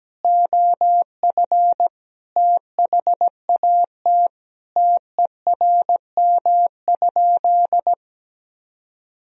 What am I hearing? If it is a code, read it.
Morse code, heard as a keyed tone, about 17 wpm, 700 Hz. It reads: OF THAT TERM?
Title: Sentences from Top 1000 Words - Review Part 4 - 17wpm